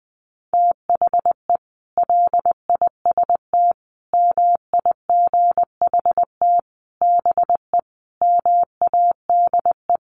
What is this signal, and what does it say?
Morse code, heard as a keyed tone, about 20 wpm, 700 Hz. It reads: THE LIST MIGHT BE MADE
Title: Sentences from Top 1100 Words 20wpm